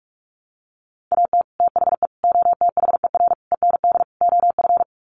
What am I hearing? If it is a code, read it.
Morse code, heard as a keyed tone, about 44 wpm, 700 Hz. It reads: AT THE OTHER END OF
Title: Sets of 5 Words 45wpm